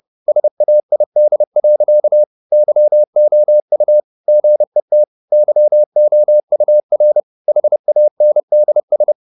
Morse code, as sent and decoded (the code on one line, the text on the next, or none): ... .- .. -.. .-.-.- / -.-- --- ..- / --. - / -.-- --- ..- .-. / .... .- -. -.. ...
SAID. YOU GET YOUR HANDS